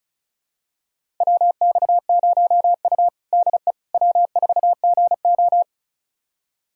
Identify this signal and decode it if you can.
Morse code, heard as a keyed tone, about 36 wpm, 700 Hz. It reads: WX0U DE W4GO
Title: USA Callsign DE Callsign - 4 Characters - 35wpm